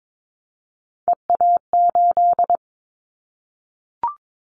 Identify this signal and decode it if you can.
Morse code, heard as a keyed tone, about 22 wpm, 700 Hz. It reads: EA8
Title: Callsign Prefixes for Top 100 DXCC - 22wpm